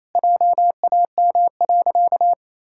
Morse code, heard as a keyed tone, about 28 wpm, 700 Hz. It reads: JAM.
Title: Pangram Challenge v2 - 28wpm